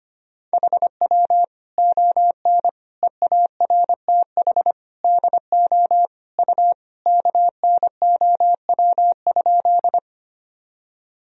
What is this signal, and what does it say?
Morse code, heard as a keyed tone, about 25 wpm, 700 Hz. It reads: HW ON EARTH DO U KNOW?
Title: Sentences from Top 200 Words and Abbreviations 25wpm